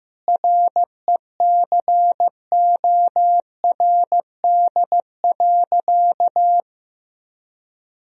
Morse code, heard as a keyed tone, about 15 wpm, 700 Hz. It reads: RECORD.
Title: Sentences from Top 1000 Words - Review Part 3 - 15wpm